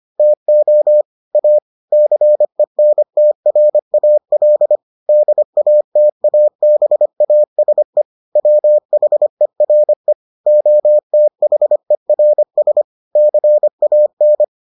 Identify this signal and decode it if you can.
Morse code, heard as a keyed tone, about 25 wpm, 600 Hz. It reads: TO A CENTRAL DATABASE WHERE OTHERS CAN